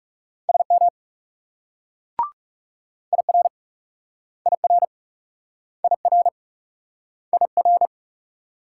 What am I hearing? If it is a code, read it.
Morse code, heard as a keyed tone, about 38 wpm, 700 Hz. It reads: SM IR IR IR IR